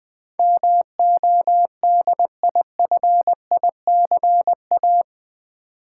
Morse code, heard as a keyed tone, about 20 wpm, 700 Hz. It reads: MODIFICA
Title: Build A Word - 4 Letters At A Time - 12-Letter Words - 20wpm